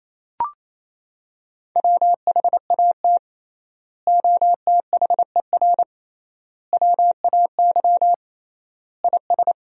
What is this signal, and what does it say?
Morse code, heard as a keyed tone, about 28 wpm, 700 Hz. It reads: WHAT OTHER WAY IS
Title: 3x Word-Spacing - Sentences from Top 100 Words - 28wpm